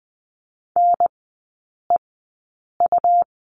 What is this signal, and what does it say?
Morse code, heard as a keyed tone, about 20 wpm, spaced as for 8 wpm, 700 Hz. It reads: NEU
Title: 121 Lesson 20e - Number 3 - Three Characters 20wpm